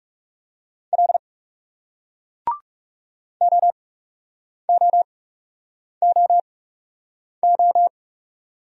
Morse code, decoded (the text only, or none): L OOOO